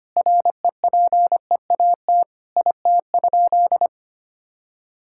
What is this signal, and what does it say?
Morse code, heard as a keyed tone, about 25 wpm, 700 Hz. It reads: REPEAT IT?
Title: Sentences from Top 1600 Words 25wpm